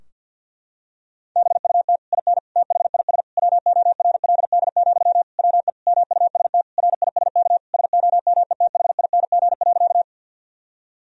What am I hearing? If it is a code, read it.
Morse code, heard as a keyed tone, about 50 wpm, 700 Hz. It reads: BUT IN THIS WORLD, WE MUST RISK SOMETHING.
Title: Sentences from Top 1000 Words - Review Part 4 - 50wpm